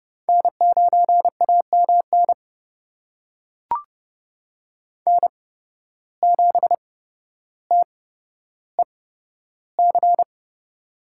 Morse code, heard as a keyed tone, about 30 wpm, spaced as for 13 wpm, 700 Hz. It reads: N9AMN N7TEC
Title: USA Call Signs - 5 Characters - 8x Character Spacing - 30wpm